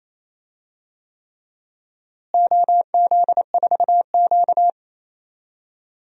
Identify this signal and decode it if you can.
Morse code, heard as a keyed tone, about 28 wpm, 700 Hz. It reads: OZ4Q
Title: DXCC-100 Call Signs - 4 Characters - Repeated 3x - 28wpm